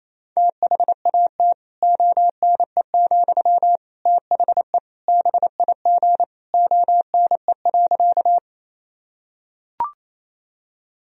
Morse code, heard as a keyed tone, about 28 wpm, 700 Hz. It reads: THAT ONE, THE BIG ONE.
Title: Sentences from Top 1000 Words - Review Part 1 - 28wpm